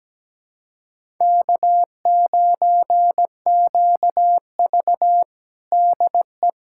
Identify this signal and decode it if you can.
Morse code, heard as a keyed tone, about 17 wpm, 700 Hz. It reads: K9QV DE
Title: USA Callsign DE Callsign - 4 Characters - 17wpm